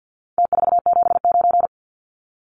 Morse code, heard as a keyed tone, about 50 wpm, 700 Hz. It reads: T479
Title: Numbers - Contest Style - 4-Digit 50wpm